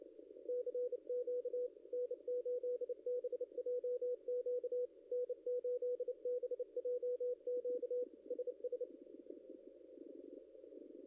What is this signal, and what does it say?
Morse code, heard as a keyed tone, about 29 wpm, 500 Hz. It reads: CQ N8BJQ N8BJQ SS
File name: contests / ss-cw-k5tr-op-n5rz -0600